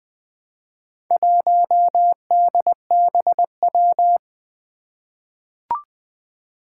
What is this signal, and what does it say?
Morse code, heard as a keyed tone, about 20 wpm, 700 Hz. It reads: 1DBW
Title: Four Challenging Letter-Number Combinations 20wpm